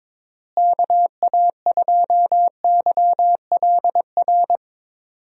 Morse code, heard as a keyed tone, about 22 wpm, 700 Hz. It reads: KA2YLR
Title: USA Call Signs - 6 Characters - Repeated 3x - 22wpm